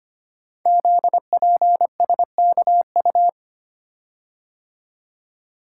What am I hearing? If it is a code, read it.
Morse code, heard as a keyed tone, about 25 wpm, 700 Hz. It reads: ZPSKU
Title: Code Groups - 5 Letter:Numbers - 25wpm